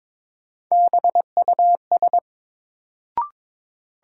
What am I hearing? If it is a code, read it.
Morse code, heard as a keyed tone, about 22 wpm, 700 Hz. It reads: BUS
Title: Sentences from Top 1000 Words - Review Part 6 - New Word Drill - 22wpm